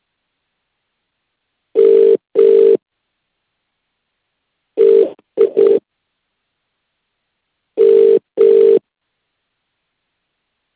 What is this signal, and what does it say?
Ringback tone (UK), 400 + 450 Hz.